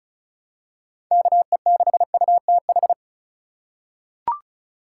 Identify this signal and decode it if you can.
Morse code, heard as a keyed tone, about 34 wpm, 700 Hz. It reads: KE6UTH
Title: USA Call Signs - 6 Characters - 2x Character Spacing - 35wpm